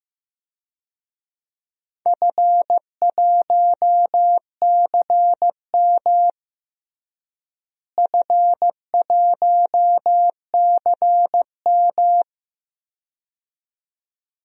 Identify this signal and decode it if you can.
Morse code, heard as a keyed tone, about 15 wpm, 700 Hz. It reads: F1CM F1CM
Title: DXCC-100 Call Signs - 4 Characters - Repeated 2x - 15wpm